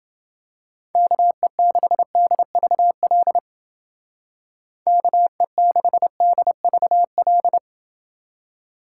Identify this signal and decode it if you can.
Morse code, heard as a keyed tone, about 29 wpm, 700 Hz. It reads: KE6DVL KE6DVL